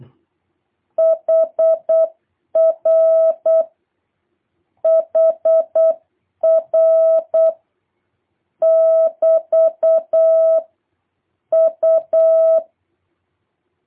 Morse code, heard as a keyed tone, about 8 wpm, 650 Hz. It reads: HR HR <BT> U